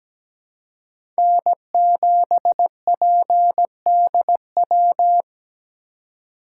Morse code, heard as a keyed tone, about 17 wpm, 700 Hz. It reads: N7PDW
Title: USA Call Signs - 5 Characters - 2x Character Spacing - 17wpm